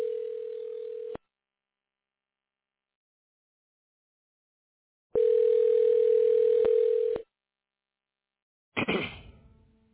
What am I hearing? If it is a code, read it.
Ringback tone (North America), 440 + 480 Hz.